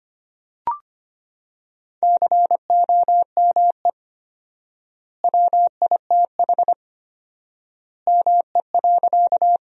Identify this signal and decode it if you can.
Morse code, heard as a keyed tone, about 25 wpm, 700 Hz. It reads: COME WITH ME.
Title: 4x Word-Spacing - Sentences from Top 200 Words - 25wpm